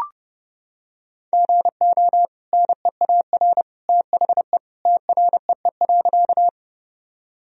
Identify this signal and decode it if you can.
Morse code, heard as a keyed tone, about 30 wpm, 700 Hz. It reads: GO NEAR THE TREE.
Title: Sentences from Top 1000 Words - Review Part 1 - 30wpm